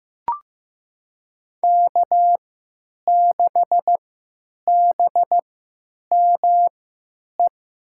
Morse code, heard as a keyed tone, about 15 wpm, spaced as for 9 wpm, 700 Hz. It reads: K6BME
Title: USA Call Signs - 3x Character Spacing - 15wpm